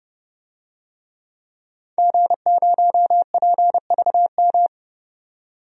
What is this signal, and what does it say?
Morse code, heard as a keyed tone, about 31 wpm, 700 Hz. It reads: G0PVM
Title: DXCC-100 Call Signs - 5 Characters - 30wpm